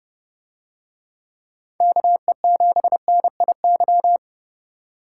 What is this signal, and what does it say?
Morse code, heard as a keyed tone, about 29 wpm, 700 Hz. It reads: KE7NIY